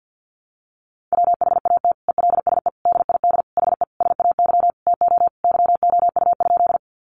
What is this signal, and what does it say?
Morse code, heard as a keyed tone, about 50 wpm, 700 Hz. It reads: WHAT ELSE DID HE SAY TO YOU?
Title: Sentences from Top 800 Words 50wpm